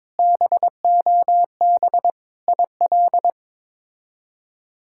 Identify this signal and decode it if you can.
Morse code, heard as a keyed tone, about 22 wpm, 700 Hz. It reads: BOB IL
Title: K1USN SST QSO Practice - 22wpm